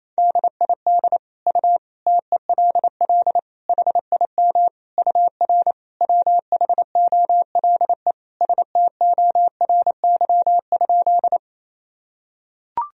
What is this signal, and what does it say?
Morse code, heard as a keyed tone, about 28 wpm, 700 Hz. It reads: DID U TELL HIM UR WHOLE STORY?
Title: Sentences from Top 400 Words and Abbreviations 28wpm